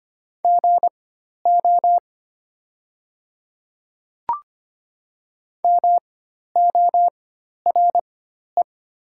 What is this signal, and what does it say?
Morse code, heard as a keyed tone, about 25 wpm, spaced as for 12 wpm, 690 Hz. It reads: GO MORE
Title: 4x Character Spacing - Top 100 Words - 25wpm